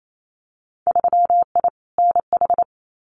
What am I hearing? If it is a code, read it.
Morse code, heard as a keyed tone, about 28 wpm, 700 Hz. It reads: 3I NH
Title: Winter Field Day Exchanges Encore 28wpm